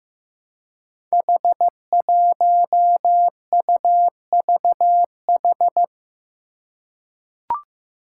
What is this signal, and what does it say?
Morse code, heard as a keyed tone, about 15 wpm, 700 Hz. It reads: H1UVH
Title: Five Challenging Letter-Number Combinations 15wpm